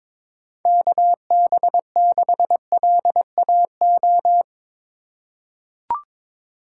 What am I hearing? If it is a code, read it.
Morse code, heard as a keyed tone, about 22 wpm, 700 Hz. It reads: KB6LAO
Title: USA Call Signs - 6 Characters - 6x Character Spacing - 22wpm